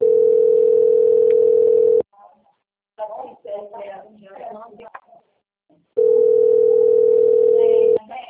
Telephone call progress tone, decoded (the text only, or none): ringback tone